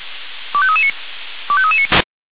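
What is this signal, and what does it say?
Five-tone selective call (ZVEI), heard as quick runs of five tones, tones of about 70 ms. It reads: 25339 25339
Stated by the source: ZVEI five-tone selective call received with an SDR.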